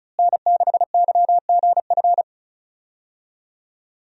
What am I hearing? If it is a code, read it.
Morse code, heard as a keyed tone, about 36 wpm, 700 Hz. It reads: N6YGF
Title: USA Call Signs - 5 Characters - Repeated 3x - 35wpm